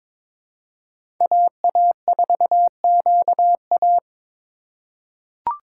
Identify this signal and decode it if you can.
Morse code, heard as a keyed tone, about 22 wpm, 700 Hz. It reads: AA4QA
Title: USA Call Signs - 5 Characters - 4x Character Spacing - 22wpm